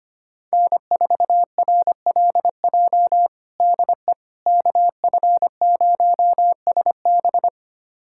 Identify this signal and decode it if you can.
Morse code, heard as a keyed tone, about 25 wpm, 700 Hz. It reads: N4RLJ DE KF0SB
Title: USA Callsign DE Callsign - 5 Characters - 25wpm